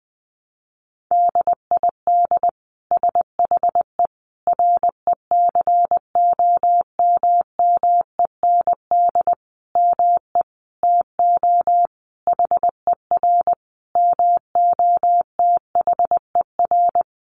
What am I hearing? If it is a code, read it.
Morse code, heard as a keyed tone, about 19 wpm, 700 Hz. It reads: DID SHE RECOMMEND ME TO HER MOTHER
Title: Sentences from Top 1200 Words 20wpm